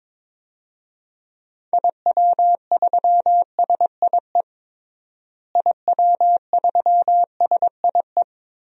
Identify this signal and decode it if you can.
Morse code, heard as a keyed tone, about 21 wpm, 700 Hz. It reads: IW3SIE IW3SIE
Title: DXCC-100 Call Signs - 6 Characters - Repeated 3x - 22wpm